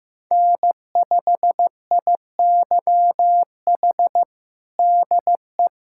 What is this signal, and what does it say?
Morse code, heard as a keyed tone, about 15 wpm, 700 Hz. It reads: N5IYH DE
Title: USA Callsign DE Callsign - 5 Characters - 15wpm